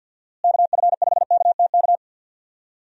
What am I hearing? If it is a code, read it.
Morse code, heard as a keyed tone, about 51 wpm, 700 Hz. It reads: KF5KTK
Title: USA Call Signs - 6 Characters - Repeated 3x - 50wpm